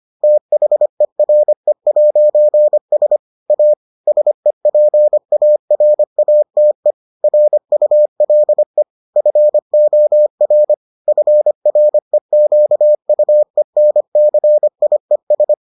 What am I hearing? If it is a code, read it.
Morse code, heard as a keyed tone, about 25 wpm, 600 Hz. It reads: THERE'S A SEPARATE RULE FOR FREQUENCIES